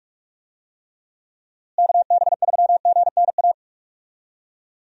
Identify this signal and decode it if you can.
Morse code, heard as a keyed tone, about 45 wpm, 700 Hz. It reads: KB3GNA
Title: USA Call Signs - 6 Characters - Repeated 2x - 45wpm